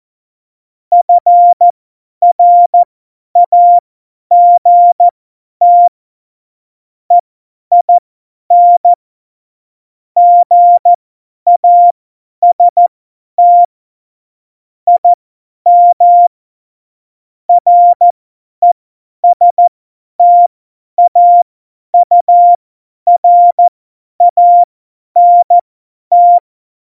Morse code, heard as a keyed tone, about 14 wpm, spaced as for 10 wpm, 700 Hz. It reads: FRAGT EIN GAST IM RESTAURANT
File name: Morsekurs / Witze / Witz 10.Dezember 2019, 50 F70